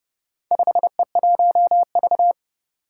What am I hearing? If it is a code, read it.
Morse code, heard as a keyed tone, about 30 wpm, 700 Hz. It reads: 5E1V
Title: Four Challenging Letter-Number Combinations 30wpm